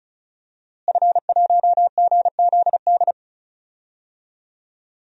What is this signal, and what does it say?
Morse code, heard as a keyed tone, about 34 wpm, 700 Hz. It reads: F1GZD
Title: DXCC-100 Call Signs - 5 Characters - Repeated 2x - 35wpm